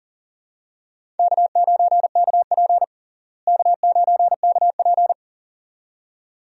Morse code, heard as a keyed tone, about 40 wpm, 700 Hz. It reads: K9KP K9KP
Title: USA Call Signs - 4 Characters - Repeated 2x - 40wpm